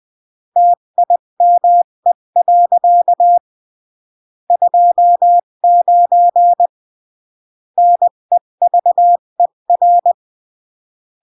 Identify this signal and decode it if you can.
Morse code, heard as a keyed tone, about 20 wpm, 700 Hz. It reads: TIME. 29 NEVER
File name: CW / jodorowsky / 82 commandments 20-15